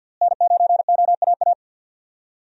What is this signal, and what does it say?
Morse code, heard as a keyed tone, about 52 wpm, 700 Hz. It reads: N9OAA
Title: USA Call Signs - 5 Characters - Repeated 2x - 50wpm